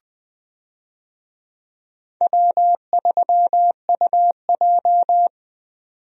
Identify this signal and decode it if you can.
Morse code, heard as a keyed tone, about 20 wpm, 700 Hz. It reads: W3UJ
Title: USA Call Signs - 4 Characters - 20wpm